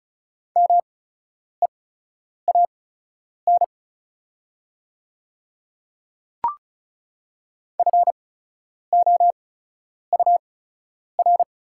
Morse code, heard as a keyed tone, about 36 wpm, spaced as for 10 wpm, 700 Hz. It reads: MEAN FOUR
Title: 8x Character Spacing - Top 300 Words - 35wpm